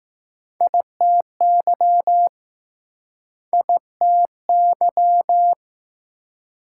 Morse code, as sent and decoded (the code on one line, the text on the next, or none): .. - -.-- / .. - -.--
ITY ITY